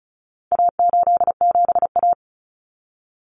Morse code, heard as a keyed tone, about 34 wpm, 700 Hz. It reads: A87A
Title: Common Cut Numbers - 4-Digit 35wpm